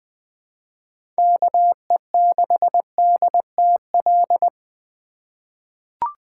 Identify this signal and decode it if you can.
Morse code, heard as a keyed tone, about 20 wpm, 700 Hz. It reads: KE6DTL
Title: USA Call Signs - 6 Characters - 3x Character Spacing - 20wpm